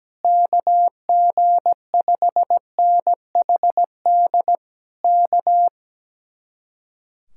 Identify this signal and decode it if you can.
Morse code, heard as a keyed tone, about 17 wpm, 700 Hz. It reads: KG5NHD K
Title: Speed-Racing Common QSO Phrases - 17wpm